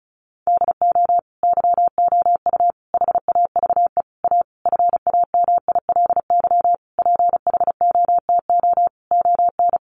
Morse code, heard as a keyed tone, about 34 wpm, 700 Hz. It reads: DO YOU HAVE A FAMILY PHOTO ON
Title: Sentences from Top 1200 Words 35wpm